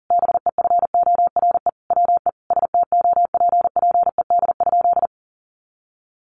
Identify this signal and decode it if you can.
Morse code, heard as a keyed tone, about 37 wpm, 700 Hz. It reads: BEFORE WE STOPPED?
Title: A Sound Of Thunder 40wpm